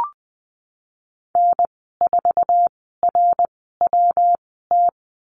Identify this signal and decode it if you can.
Morse code, heard as a keyed tone, about 20 wpm, spaced as for 15 wpm, 700 Hz. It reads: N4RWT
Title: 147 Lesson 24g - Letter y - Call Signs 20wpm